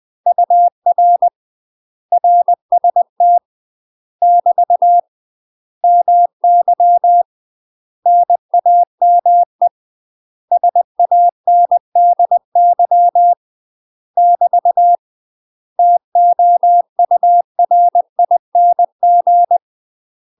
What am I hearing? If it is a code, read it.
Morse code, heard as a keyed tone, about 20 wpm, 700 Hz. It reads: UR RST <BT> MY NAME SANDY <BT> TOURING
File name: UK MASTER QSOs TRAINING / 20WPM / UK MASTER QSOs 0073